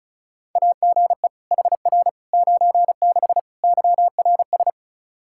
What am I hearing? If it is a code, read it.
Morse code, heard as a keyed tone, about 36 wpm, 700 Hz. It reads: AGE HR 96 YRS
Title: Practice QSO - Exchange 3 - More Info - 35wpm